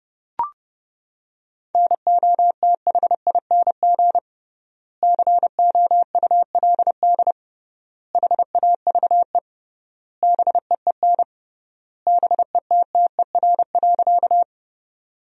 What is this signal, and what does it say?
Morse code, heard as a keyed tone, about 30 wpm, 700 Hz. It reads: NOTHING COULD HAVE BEEN BETTER.
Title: 3x Word-Spacing - Sentences from Top 500 Words - 30wpm